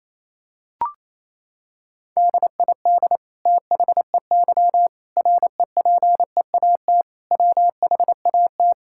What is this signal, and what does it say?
Morse code, heard as a keyed tone, about 28 wpm, 700 Hz. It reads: DID THEY REPEAT WHAT
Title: Sentences from Top 1600 Words 28wpm